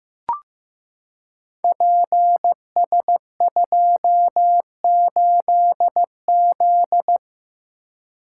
Morse code, heard as a keyed tone, about 15 wpm, 700 Hz. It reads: PS28Z
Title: Code Groups - 5 Letter:Numbers - 15wpm